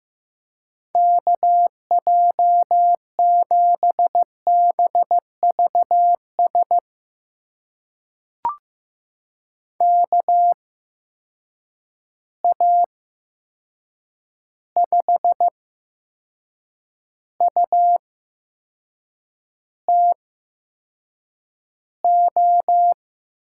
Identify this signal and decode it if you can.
Morse code, heard as a keyed tone, about 15 wpm, spaced as for 6 wpm, 700 Hz. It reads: KJ7BVS KA5UTO
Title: USA Call Signs - 6 Characters - 8x Character Spacing - 15wpm